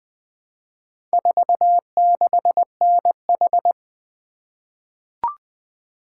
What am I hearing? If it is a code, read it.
Morse code, heard as a keyed tone, about 20 wpm, 700 Hz. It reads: 46NH
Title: Four Letter-Number Combinations - 20wpm